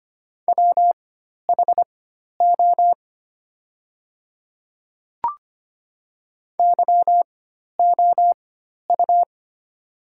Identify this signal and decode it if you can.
Morse code, heard as a keyed tone, about 25 wpm, spaced as for 12 wpm, 700 Hz. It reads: WHO YOU
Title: 4x Character Spacing - Top 100 Words - 25wpm